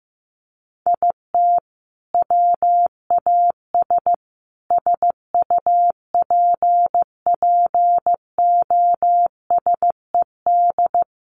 Morse code, heard as a keyed tone, about 15 wpm, 700 Hz. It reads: IT WAS SUPPOSED
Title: Sets of 5 Words Encore 15wpm